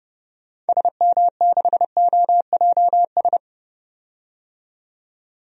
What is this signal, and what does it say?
Morse code, heard as a keyed tone, about 30 wpm, 700 Hz. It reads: SM6OJS
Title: DXCC-100 Call Signs - 6 Characters - 30wpm